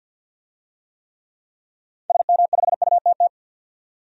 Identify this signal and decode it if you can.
Morse code, heard as a keyed tone, about 46 wpm, 700 Hz. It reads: SM5UTT
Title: DXCC-100 Call Signs - 6 Characters - 50wpm